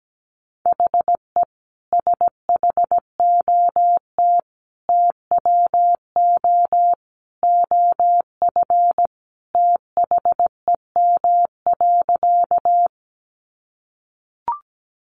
Morse code, heard as a keyed tone, about 17 wpm, 700 Hz. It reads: HE SHOT TWO OF THEM.
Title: Sentences from Top 1100 Words 17wpm